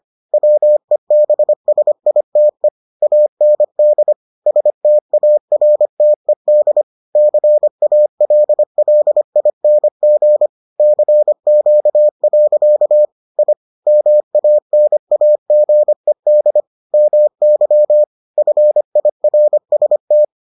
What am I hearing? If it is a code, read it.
Morse code, heard as a keyed tone, about 25 wpm, 600 Hz. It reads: WEBSITE AND STARTED CALLING CQ. I MANAGED MY FIRST